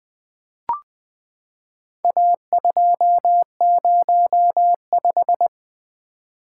Morse code, heard as a keyed tone, about 20 wpm, 700 Hz. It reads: A205
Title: Four Letter-Number Combinations - 20wpm